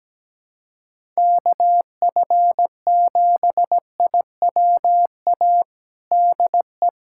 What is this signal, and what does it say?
Morse code, heard as a keyed tone, about 17 wpm, 700 Hz. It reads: KF7IWA DE
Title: USA Callsign DE Callsign 17wpm